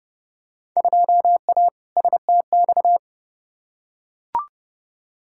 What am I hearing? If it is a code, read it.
Morse code, heard as a keyed tone, about 31 wpm, 700 Hz. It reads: 2A STX
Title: Field Day Exchanges - 30wpm